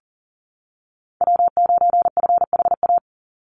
Morse code, heard as a keyed tone, about 40 wpm, 700 Hz. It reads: W9FHA
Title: NA SOTA Chaser Callsigns 40wpm